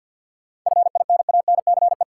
Morse code, heard as a keyed tone, about 49 wpm, 700 Hz. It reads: FINANCE
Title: Sentences from Top 1500 Words - New Word Drill - 50wpm